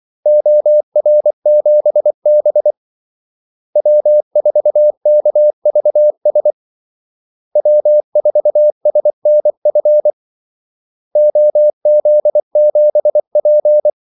Morse code, heard as a keyed tone, about 24 wpm, 600 Hz. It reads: OR7B W4KVS W4SNF OZ7P
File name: Callsigns-24wpm